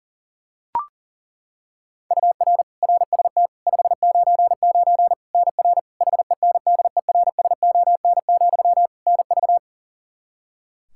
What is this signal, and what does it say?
Morse code, heard as a keyed tone, about 37 wpm, 700 Hz. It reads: UR RST 599 NR HENDERSON, NV
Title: Practice QSO - Exchange 1 - Beginning - 40wpm